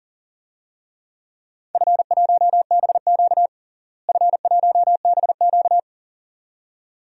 Morse code, heard as a keyed tone, about 40 wpm, 700 Hz. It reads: F1BQ F1BQ
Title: DXCC-100 Call Signs - 4 Characters - Repeated 2x - 40wpm